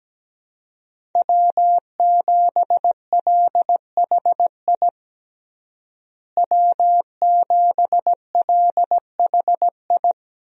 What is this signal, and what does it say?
Morse code, heard as a keyed tone, about 17 wpm, 700 Hz. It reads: W7LHI W7LHI
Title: USA Call Signs - 5 Characters - Repeated 3x - 17wpm